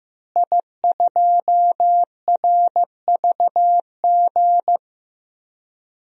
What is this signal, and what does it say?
Morse code, heard as a keyed tone, about 15 wpm, 700 Hz. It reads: I2RVG
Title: DXCC-100 Call Signs - 5 Characters - Repeated 3x - 15wpm